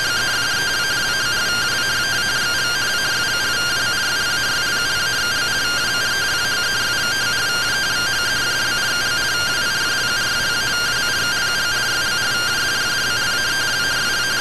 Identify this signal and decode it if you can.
Two-tone teleprinter signal, about 170 Hz shift.
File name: findings / CT AXIAL -572mm 2.gen-15 rtty rx